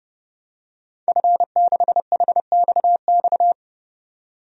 Morse code, heard as a keyed tone, about 30 wpm, 700 Hz. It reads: F6HXX
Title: DXCC-100 Call Signs - 5 Characters - 30wpm